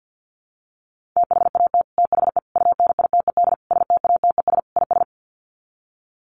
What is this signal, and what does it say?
Morse code, heard as a keyed tone, about 50 wpm, 700 Hz. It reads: THAT THE UNITED STATES IS
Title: Sets of 5 Words Encore 50wpm